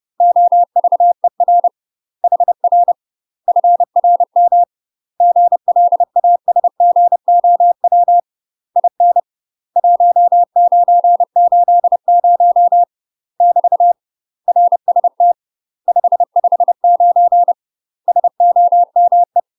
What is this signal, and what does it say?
Morse code, heard as a keyed tone, about 30 wpm, 700 Hz. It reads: OVER HR FRM GLASGOW IN 1980 <BT> RST 559 SOME
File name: UK MASTER QSOs TRAINING / 30WPM / UK MASTER QSOs 0013